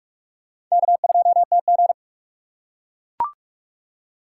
Morse code, heard as a keyed tone, about 41 wpm, 690 Hz. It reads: K2TG
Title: USA Call Signs - 4 Characters - 3x Character Spacing - 45wpm